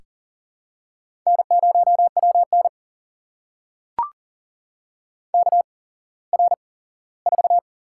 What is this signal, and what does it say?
Morse code, heard as a keyed tone, about 39 wpm, spaced as for 19 wpm, 700 Hz. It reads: N0WN KR4
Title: USA Call Signs - 4 Characters - 8x Character Spacing - 40wpm